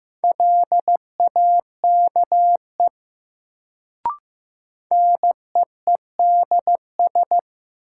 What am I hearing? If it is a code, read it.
Morse code, heard as a keyed tone, about 15 wpm, 700 Hz. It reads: LAKE NEEDS
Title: Sentences from Top 1000 Words - Review Part 3 - New Word Drill - 15wpm